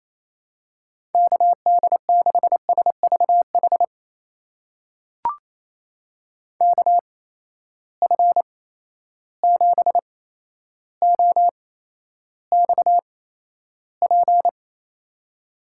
Morse code, heard as a keyed tone, about 28 wpm, spaced as for 12 wpm, 700 Hz. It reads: KD6SVH KF7OXP
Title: USA Call Signs - 6 Characters - 8x Character Spacing - 28wpm